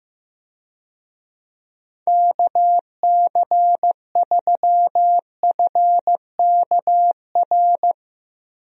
Morse code, heard as a keyed tone, about 15 wpm, 700 Hz. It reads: KC3FKR
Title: USA Call Signs - 6 Characters - Repeated 2x - 15wpm